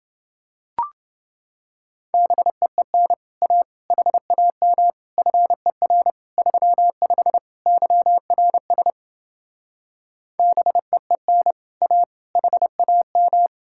Morse code, heard as a keyed tone, about 29 wpm, 700 Hz. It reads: BEEN A HAM FER 35 YRS BEEN A HAM